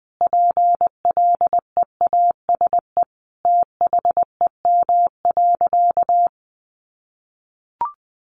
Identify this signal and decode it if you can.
Morse code, heard as a keyed tone, about 20 wpm, 700 Hz. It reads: PLEASE THEM.
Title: Sentences from Top 1500 Words 20wpm